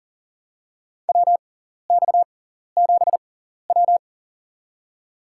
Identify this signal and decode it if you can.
Morse code, heard as a keyed tone, about 40 wpm, spaced as for 14 wpm, 700 Hz. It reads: WX7W
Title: USA Call Signs - 4 Characters - 6x Character Spacing - 40wpm